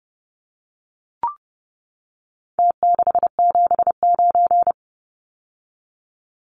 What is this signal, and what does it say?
Morse code, heard as a keyed tone, about 30 wpm, 700 Hz. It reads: T679
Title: Numbers - Contest Style - 4-Digit 30wpm